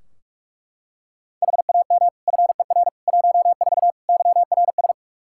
Morse code, heard as a keyed tone, about 42 wpm, 700 Hz. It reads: HAM FER 14 YRS
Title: Practice QSO - Exchange 3 - More Info - 45wpm